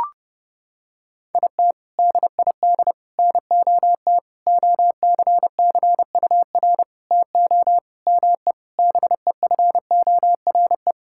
Morse code, heard as a keyed tone, about 32 wpm, 700 Hz. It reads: IT DID NOT OCCUR TO ME BEFORE
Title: Sentences from Top 1000 Words - Review Part 6 - 30wpm